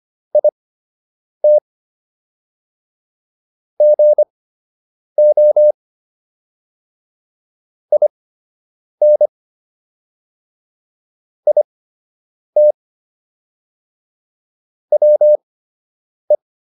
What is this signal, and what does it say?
Morse code, heard as a keyed tone, about 25 wpm, spaced as for 8 wpm, 600 Hz. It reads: IT GO IN IT WE